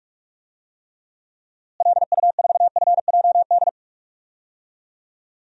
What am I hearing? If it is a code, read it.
Morse code, heard as a keyed tone, about 41 wpm, 700 Hz. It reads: LU4FJD
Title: DXCC-100 Call Signs - 6 Characters - Repeated 3x - 45wpm